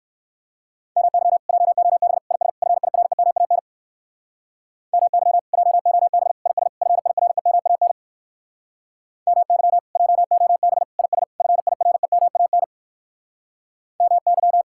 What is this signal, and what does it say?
Morse code, heard as a keyed tone, about 58 wpm, 700 Hz. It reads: MY JOB IS FIREMAN MY JOB IS FIREMAN MY JOB IS FIREMAN MY